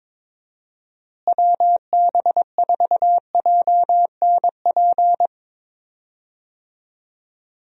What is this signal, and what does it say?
Morse code, heard as a keyed tone, about 22 wpm, 700 Hz. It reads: WB4JNP
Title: USA Call Signs - 6 Characters - Repeated 3x - 22wpm